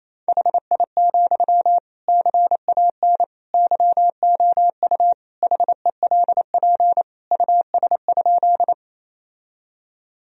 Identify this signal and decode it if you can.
Morse code, heard as a keyed tone, about 28 wpm, 700 Hz. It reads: HI, CAN YOU HELP US?
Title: Sentences from Top 1000 Words - Review Part 3 - 28wpm